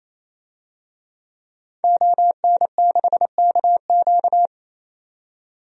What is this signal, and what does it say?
Morse code, heard as a keyed tone, about 27 wpm, 700 Hz. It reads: ON6KQ
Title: DXCC-100 Call Signs - 5 Characters - 28wpm